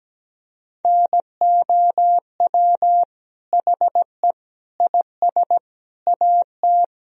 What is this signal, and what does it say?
Morse code, heard as a keyed tone, about 17 wpm, 700 Hz. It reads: NOW HE IS AT
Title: Sentences from Top 1000 Words - Review Part 1 - 17wpm